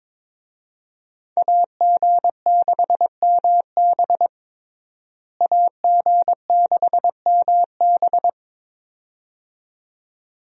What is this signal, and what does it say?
Morse code, heard as a keyed tone, about 22 wpm, 700 Hz. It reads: AG6MB AG6MB
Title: USA Call Signs - 5 Characters - Repeated 2x - 22wpm